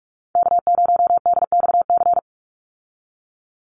Morse code, heard as a keyed tone, about 44 wpm, 700 Hz. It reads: K0DXC
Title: Call Signs Repeated 3x - 45wpm